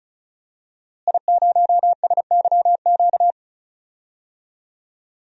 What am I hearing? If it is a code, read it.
Morse code, heard as a keyed tone, about 34 wpm, 700 Hz. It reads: I0SYQ
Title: DXCC-100 Call Signs - 5 Characters - Repeated 3x - 35wpm